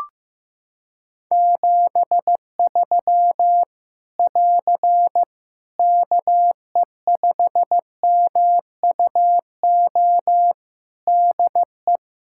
Morse code, heard as a keyed tone, about 15 wpm, 700 Hz. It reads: 73 <AR> KE5MUO DE
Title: Practice QSO - Exchange 4 - Ending - 15wpm